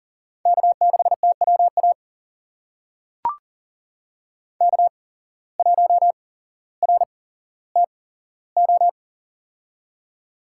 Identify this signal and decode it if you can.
Morse code, heard as a keyed tone, about 40 wpm, spaced as for 17 wpm, 700 Hz. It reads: K6TWA K1RTO
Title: USA Call Signs - 5 Characters - 8x Character Spacing - 40wpm